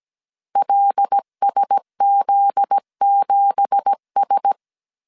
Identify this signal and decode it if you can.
Morse code, heard as a keyed tone, about 17 wpm, 800 Hz. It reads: LSZ7S
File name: CW / DET / 1 9 1500C S A 16wpm 2828